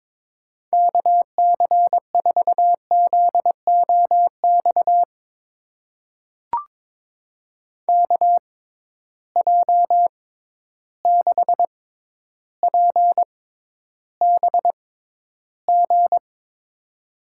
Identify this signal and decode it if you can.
Morse code, heard as a keyed tone, about 22 wpm, spaced as for 11 wpm, 700 Hz. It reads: KC4ZOX KJ6PBG
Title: USA Call Signs - 6 Characters - 6x Character Spacing - 22wpm